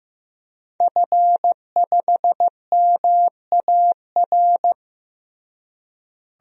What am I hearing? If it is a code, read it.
Morse code, heard as a keyed tone, about 15 wpm, 700 Hz. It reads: F5MAR
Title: DXCC-100 Call Signs - 5 Characters - Repeated 2x - 15wpm